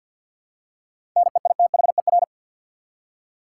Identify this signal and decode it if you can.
Morse code, heard as a keyed tone, about 48 wpm, 700 Hz. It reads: NEITHER
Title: Seven Letter Words - 50wpm